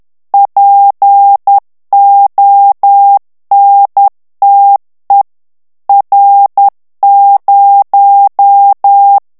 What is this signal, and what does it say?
Morse code, heard as a keyed tone, about 10 wpm, 800 Hz. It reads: PONTE R0
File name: morse mp3 / ITALIEN texte clair 1 vitesse 10 mots minute